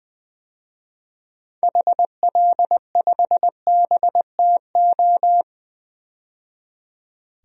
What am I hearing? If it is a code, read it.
Morse code, heard as a keyed tone, about 21 wpm, 700 Hz. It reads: HL5BTO
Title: DXCC-100 Call Signs - 6 Characters - 20wpm